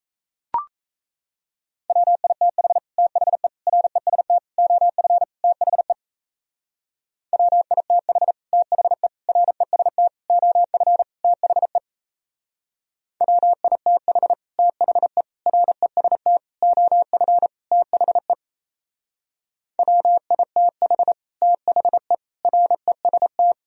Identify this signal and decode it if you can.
Morse code, heard as a keyed tone, about 34 wpm, 700 Hz. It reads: WITH THE REST OF THE WITH THE REST OF THE WITH THE REST OF THE WITH THE REST
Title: Speed-Racing - Sets of 5 Words 28wpm